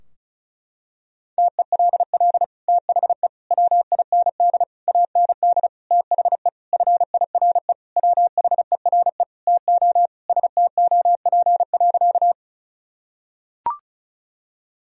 Morse code, heard as a keyed tone, about 34 wpm, 700 Hz. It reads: TELL THE WIND AND THE FIRE WHERE TO STOP.
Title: Sentences from Top 1000 Words - Review Part 2 - 35wpm